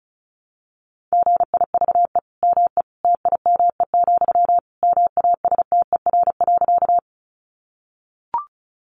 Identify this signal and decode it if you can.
Morse code, heard as a keyed tone, about 34 wpm, 700 Hz. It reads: GIVE ME TIME, MASTER.
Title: Sentences from Top 1300 Words 35wpm